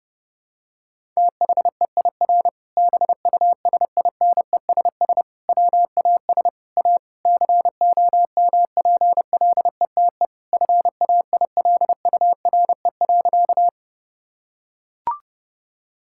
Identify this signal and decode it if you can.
Morse code, heard as a keyed tone, about 32 wpm, 700 Hz. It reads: THEIR BUSINESS WAS A COMPLETE FAILURE.
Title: Sentences from Top 1000 Words - Review Part 6 - 30wpm